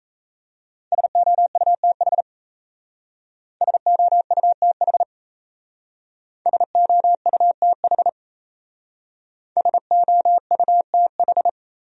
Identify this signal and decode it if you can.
Morse code, heard as a keyed tone, about 37 wpm, 700 Hz. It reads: SOUTH SOUTH SOUTH SOUTH